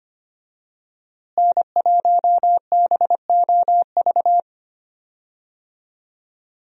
Morse code, heard as a keyed tone, about 25 wpm, 700 Hz. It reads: N1BOV
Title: USA Call Signs - 5 Characters - Repeated 2x - 25wpm